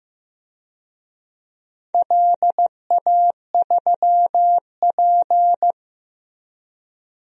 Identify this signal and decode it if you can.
Morse code, heard as a keyed tone, about 15 wpm, 700 Hz. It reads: LA3P
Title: DXCC-100 Call Signs - 4 Characters - Repeated 2x - 15wpm